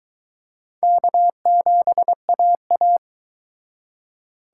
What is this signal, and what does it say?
Morse code, heard as a keyed tone, about 23 wpm, 700 Hz. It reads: K7AA